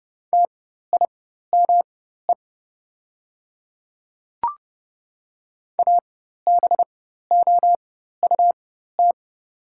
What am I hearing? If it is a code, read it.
Morse code, heard as a keyed tone, about 30 wpm, spaced as for 14 wpm, 700 Hz. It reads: TIME ABOUT
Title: 4x Character Spacing - Top 100 Words - 30wpm